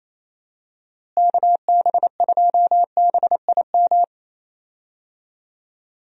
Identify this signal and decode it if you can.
Morse code, heard as a keyed tone, about 27 wpm, 700 Hz. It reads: KB2BIM
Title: USA Call Signs - 6 Characters - Repeated 2x - 28wpm